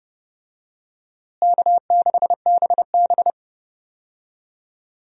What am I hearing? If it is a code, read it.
Morse code, heard as a keyed tone, about 30 wpm, 700 Hz. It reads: K6BB
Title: USA Call Signs - 4 Characters - Repeated 3x - 30wpm